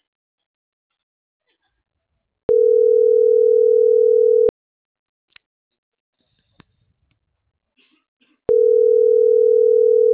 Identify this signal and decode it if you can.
Ringback tone (North America), 440 + 480 Hz.